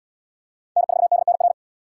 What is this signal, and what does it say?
Morse code, heard as a keyed tone, about 73 wpm, 700 Hz. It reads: N4DNR